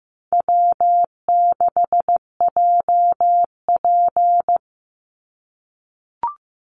Morse code, heard as a keyed tone, about 15 wpm, 700 Hz. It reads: W6JP
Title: NA SOTA Chaser Callsigns 15wpm